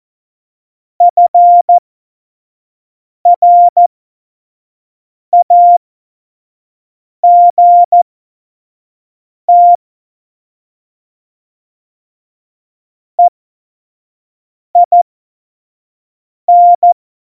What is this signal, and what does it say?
Morse code, heard as a keyed tone, about 14 wpm, spaced as for 5 wpm, 700 Hz. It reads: FRAGT EIN
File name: Morsekurs / Witze / Witz 19. Januar 2021, 25 F70